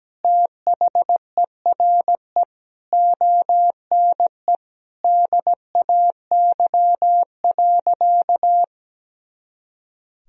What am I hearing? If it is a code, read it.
Morse code, heard as a keyed tone, about 17 wpm, 700 Hz. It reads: THERE ONE DAY.
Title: Sentences from Top 300 Words and Abbreviations 17wpm